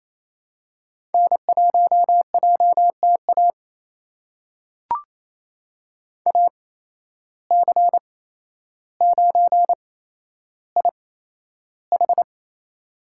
Morse code, heard as a keyed tone, about 28 wpm, spaced as for 12 wpm, 700 Hz. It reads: N1JTA AC9IH
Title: USA Call Signs - 5 Characters - 8x Character Spacing - 28wpm